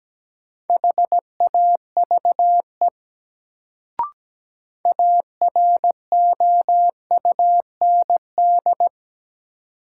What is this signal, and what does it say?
Morse code, heard as a keyed tone, about 17 wpm, 700 Hz. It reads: HAVE AROUND
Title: Sets of 2 Words - New Word Drill - 17wpm